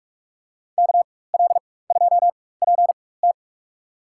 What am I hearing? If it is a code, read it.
Morse code, heard as a keyed tone, about 43 wpm, spaced as for 21 wpm, 700 Hz. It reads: KL2PT
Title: USA Call Signs - 5 Characters - 4x Character Spacing - 45wpm